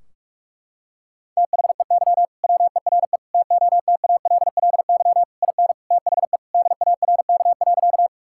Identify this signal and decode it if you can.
Morse code, heard as a keyed tone, about 44 wpm, 700 Hz. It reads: THEY WERE TOTALLY IN THE DARK.
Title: Sentences from Top 1600 Words 45wpm